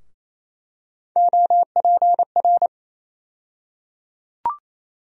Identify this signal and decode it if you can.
Morse code, heard as a keyed tone, about 28 wpm, 700 Hz. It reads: OPR
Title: USA Call Sign Postfixes - 3 Letters 28wpm